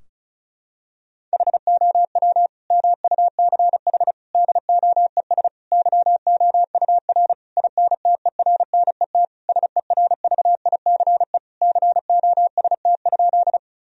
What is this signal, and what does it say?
Morse code, heard as a keyed tone, about 36 wpm, 700 Hz. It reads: HOW MUCH DOES YOUR INTERNET SERVICE COST?
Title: Sentences from Top 1000 Words - Review Part 4 - 35wpm